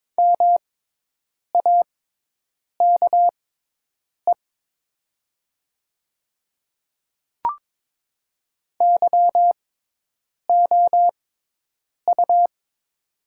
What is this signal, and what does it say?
Morse code, heard as a keyed tone, about 22 wpm, spaced as for 8 wpm, 700 Hz. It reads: MAKE YOU
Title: 6x Character Spacing - Top 100 Words - 22wpm